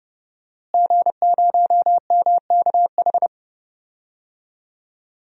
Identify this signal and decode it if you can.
Morse code, heard as a keyed tone, about 30 wpm, 700 Hz. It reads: G0MKH
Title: DXCC-100 Call Signs - 5 Characters - Repeated 2x - 30wpm